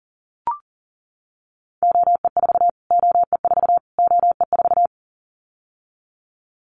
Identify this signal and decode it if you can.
Morse code, heard as a keyed tone, about 40 wpm, 700 Hz. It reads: OE4 OE4 OE4
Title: Callsign Prefixes for Top 25 DXCC - Repeated 3x - 40wpm